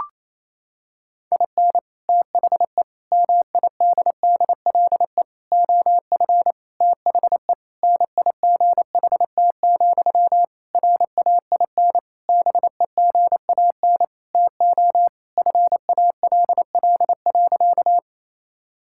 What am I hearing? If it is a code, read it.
Morse code, heard as a keyed tone, about 28 wpm, 700 Hz. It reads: IN THE MIDDLE OF THE NIGHT, RAIN BEGAN TO FALL.
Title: Sentences from Top 1000 Words - Review Part 6 - 28wpm